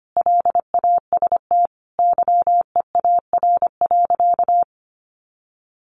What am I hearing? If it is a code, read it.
Morse code, heard as a keyed tone, about 25 wpm, 700 Hz. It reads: LAST YEAR.
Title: Sentences from Top 1200 Words 25wpm